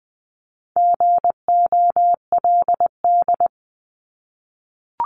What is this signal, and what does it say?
Morse code, heard as a keyed tone, about 20 wpm, 700 Hz. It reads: GOLD
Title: Top 500 Words 20wpm